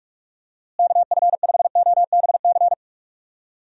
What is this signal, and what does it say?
Morse code, heard as a keyed tone, about 46 wpm, 700 Hz. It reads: KF5OBC
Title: USA Call Signs - 6 Characters - Repeated 2x - 45wpm